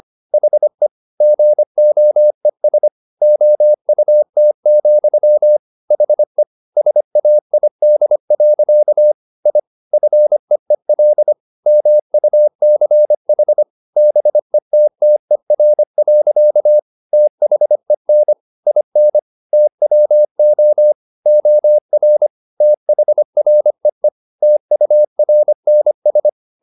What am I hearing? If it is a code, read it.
Morse code, heard as a keyed tone, about 25 wpm, 600 Hz. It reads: HE GOES OUT, HE SAID. I FEEL MUCH BETTER. THEN IN TWO OR THREE TURNS